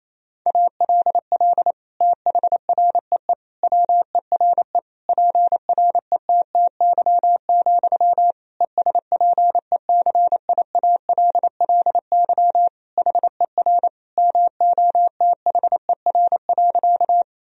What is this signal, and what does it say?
Morse code, heard as a keyed tone, about 27 wpm, 700 Hz. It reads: ALL THREE WERE PRETTY, ESPECIALLY HER MOTHER.
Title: Sentences from Top 1000 Words - Review Part 4 - 28wpm